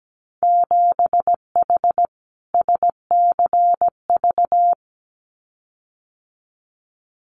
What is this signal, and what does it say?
Morse code, heard as a keyed tone, about 17 wpm, 700 Hz. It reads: 7H SCV
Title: Winter Field Day Exchanges Encore 17wpm